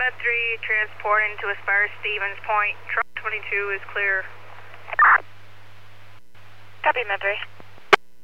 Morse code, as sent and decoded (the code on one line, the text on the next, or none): none